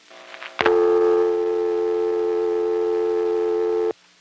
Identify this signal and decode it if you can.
Dial tone (North America), 350 + 440 Hz.
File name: phonetrips / EvanNX11HQ